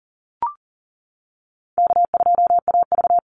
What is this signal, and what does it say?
Morse code, heard as a keyed tone, about 40 wpm, 700 Hz. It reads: K2AV
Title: Easy Call Signs - 40wpm